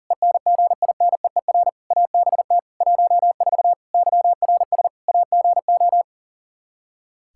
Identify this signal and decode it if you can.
Morse code, heard as a keyed tone, about 37 wpm, 700 Hz. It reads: ENGINEER ABT 14 YRS AGO
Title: K1USN SST QSO Practice 40wpm